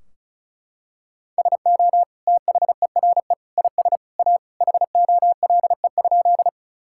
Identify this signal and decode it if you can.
Morse code, heard as a keyed tone, about 36 wpm, 700 Hz. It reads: SO THERE IS A HOLE?
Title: Sentences from Top 1600 Words 35wpm